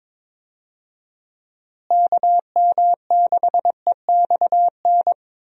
Morse code, heard as a keyed tone, about 21 wpm, 700 Hz. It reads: KM6EXN
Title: USA Call Signs - 6 Characters - Repeated 2x - 22wpm